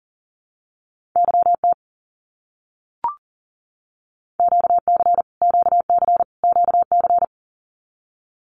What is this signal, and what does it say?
Morse code, heard as a keyed tone, about 40 wpm, 700 Hz. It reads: YT QC QC QC
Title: Canadian Provinces and Territories - Repeated 3x - 40wpm